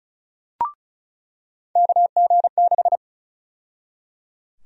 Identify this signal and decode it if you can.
Morse code, heard as a keyed tone, about 34 wpm, 700 Hz. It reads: KG6
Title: USA Call Sign Prefixes - 35wpm